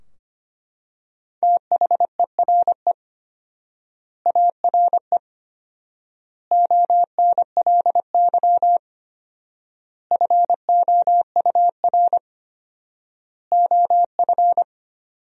Morse code, heard as a keyed tone, about 25 wpm, 700 Hz. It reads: THERE ARE ONLY FOUR OF
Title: 4x Word-Spacing - Sentences from Top 300 Words - 25wpm